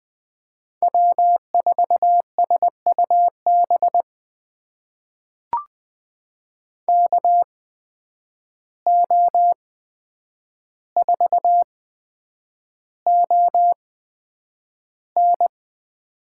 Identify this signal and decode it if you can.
Morse code, heard as a keyed tone, about 20 wpm, spaced as for 9 wpm, 700 Hz. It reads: W4SUB KO4ON
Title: USA Call Signs - 5 Characters - 8x Character Spacing - 20wpm